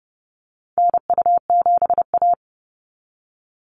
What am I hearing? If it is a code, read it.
Morse code, heard as a keyed tone, about 30 wpm, 700 Hz. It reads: NU7A
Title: NA SOTA Activator Callsigns 30wpm